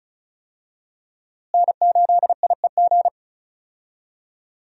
Morse code, heard as a keyed tone, about 34 wpm, 700 Hz. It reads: N8IEG
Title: USA Call Signs - 5 Characters - Repeated 2x - 35wpm